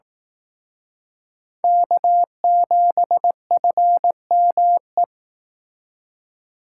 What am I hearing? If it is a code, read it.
Morse code, heard as a keyed tone, about 18 wpm, 700 Hz. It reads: K7FME
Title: Speed Racing - USA Call Signs 15wpm